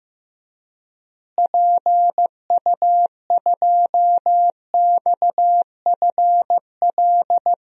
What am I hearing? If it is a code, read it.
Morse code, heard as a keyed tone, about 15 wpm, 700 Hz. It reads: PU2XFL
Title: DXCC-100 Call Signs - 6 Characters - Repeated 2x - 15wpm